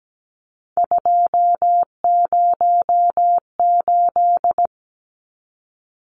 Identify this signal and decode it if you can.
Morse code, heard as a keyed tone, about 17 wpm, 700 Hz. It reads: 208
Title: Numbers - 3-Digit 17wpm